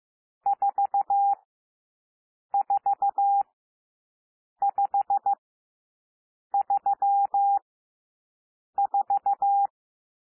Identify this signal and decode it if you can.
Morse code, heard as a keyed tone, about 15 wpm, spaced as for 6 wpm, 800 Hz. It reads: 44534
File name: SOUND NUMBERS 15 wpm